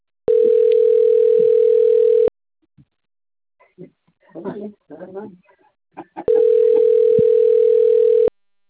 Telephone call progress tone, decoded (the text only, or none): ringback tone